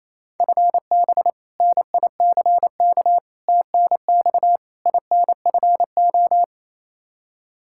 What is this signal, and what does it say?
Morse code, heard as a keyed tone, about 27 wpm, 700 Hz. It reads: FB NICK TNX INFO
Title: Practice QSO - Exchange 3 - More Info - 28wpm